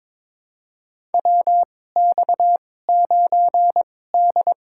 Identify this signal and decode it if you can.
Morse code, heard as a keyed tone, about 22 wpm, spaced as for 16 wpm, 700 Hz. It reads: WX9D
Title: USA Call Signs - 4 Characters - 2x Character Spacing - 22wpm